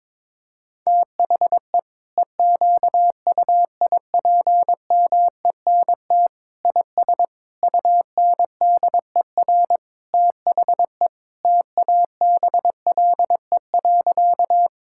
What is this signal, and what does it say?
Morse code, heard as a keyed tone, about 22 wpm, 700 Hz. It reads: THE EQUIPMENT IS UNDER THE TABLE.
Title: Sentences from Top 1000 Words - Review Part 5 - 22wpm